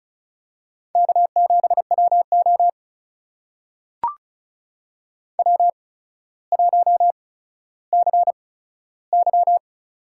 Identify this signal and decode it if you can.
Morse code, heard as a keyed tone, about 36 wpm, spaced as for 15 wpm, 700 Hz. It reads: K7WO W1CY
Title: USA Call Signs - 4 Characters - 8x Character Spacing - 35wpm